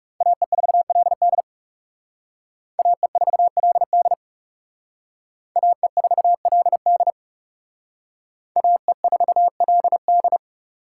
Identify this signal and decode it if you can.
Morse code, heard as a keyed tone, about 36 wpm, 700 Hz. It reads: AE4LD AE4LD AE4LD AE4LD
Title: Speed Racing - USA Call Signs 30wpm